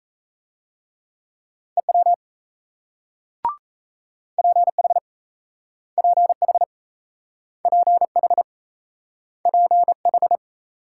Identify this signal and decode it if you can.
Morse code, heard as a keyed tone, about 38 wpm, 700 Hz. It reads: EW PH PH PH PH